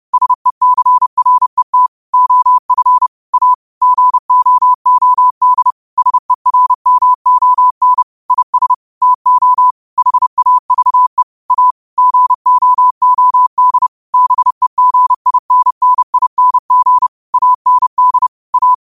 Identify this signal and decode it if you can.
Morse code, heard as a keyed tone, about 30 wpm, 1000 Hz. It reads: SECRET OF A GOOD SERMON IS TO HAVE A GOOD BEGINNING AND A